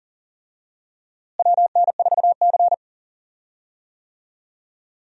Morse code, heard as a keyed tone, about 40 wpm, 700 Hz. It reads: WN4C
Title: USA Call Signs - 4 Characters - Repeated 3x - 40wpm